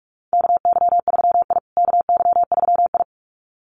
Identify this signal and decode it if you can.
Morse code, heard as a keyed tone, about 45 wpm, 700 Hz. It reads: KY3I KY3I
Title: USA Call Signs Repeated 2x 45wpm